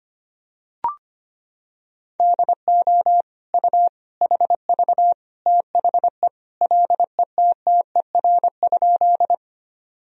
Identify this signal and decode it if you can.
Morse code, heard as a keyed tone, about 25 wpm, 700 Hz. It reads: DO U HV THE LETTER?
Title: Sentences from Top 300 Words and Abbreviations 25wpm